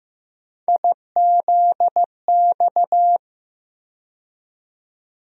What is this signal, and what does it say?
Morse code, heard as a keyed tone, about 16 wpm, 700 Hz. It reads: IZX
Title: USA Call Sign Postfixes - 3 Letters 15wpm